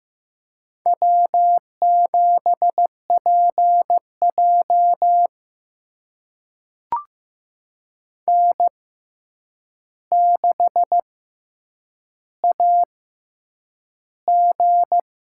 Morse code, heard as a keyed tone, about 15 wpm, spaced as for 8 wpm, 700 Hz. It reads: W7PJ N6AG
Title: USA Call Signs - 4 Characters - 6x Character Spacing - 15wpm